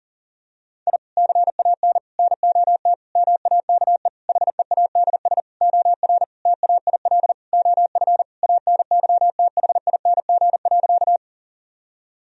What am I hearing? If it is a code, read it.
Morse code, heard as a keyed tone, about 40 wpm, 700 Hz. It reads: I CAN NOT MAKE HEADS OR TAIL OF ANYTHING.
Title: Sentences from Top 1000 Words - Review Part 3 - 40wpm